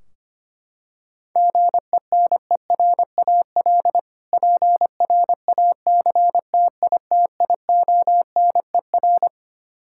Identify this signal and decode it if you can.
Morse code, heard as a keyed tone, about 25 wpm, 700 Hz. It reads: GENERAL PRACTITIONER
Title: Practice QSO - Exchange 3 - More Info - 25wpm